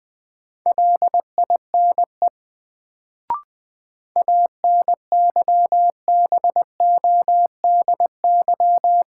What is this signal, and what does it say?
Morse code, heard as a keyed tone, about 20 wpm, 700 Hz. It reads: LINE ANYBODY
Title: Sets of 5 Words - New Word Drill - 20wpm